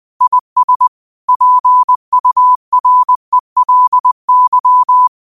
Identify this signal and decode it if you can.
Morse code, heard as a keyed tone, about 20 wpm, 1000 Hz. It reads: IS PURELY
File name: qotdfiles / 20wpm